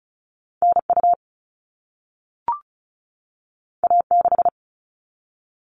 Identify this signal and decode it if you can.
Morse code, heard as a keyed tone, about 36 wpm, 690 Hz. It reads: NU A6
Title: All Cut Numbers - 2-Digit 35wpm